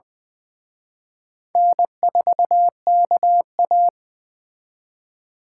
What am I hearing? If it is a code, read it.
Morse code, heard as a keyed tone, about 24 wpm, 700 Hz. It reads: N4KA
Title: Speed Racing - USA Call Signs 20wpm